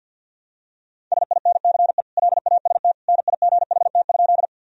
Morse code, heard as a keyed tone, about 51 wpm, 700 Hz. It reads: SINCE LAST NIGHT?
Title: Sentences from Top 1000 Words - Review Part 1 - 50wpm